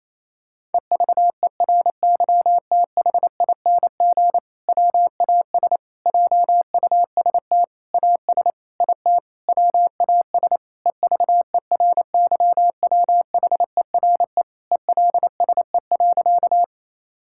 Morse code, heard as a keyed tone, about 27 wpm, 700 Hz. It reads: EVERYTHING WAS JUST AS IT WAS EVERYWHERE ELSE.
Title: Sentences from Top 1600 Words 28wpm